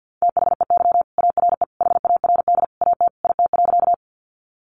Morse code, heard as a keyed tone, about 50 wpm, 700 Hz. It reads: THEY ARE HARD AT IT.
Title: Sentences from Top 300 Words 50wpm